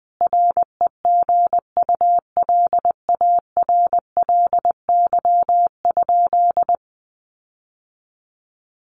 Morse code, heard as a keyed tone, about 20 wpm, 700 Hz. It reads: REGULARLY?
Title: Sentences from Top 1400 Words 20wpm